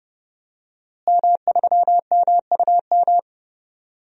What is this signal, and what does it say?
Morse code, heard as a keyed tone, about 30 wpm, 700 Hz. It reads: M3MUM
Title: DXCC-100 Call Signs - 5 Characters - Repeated 2x - 30wpm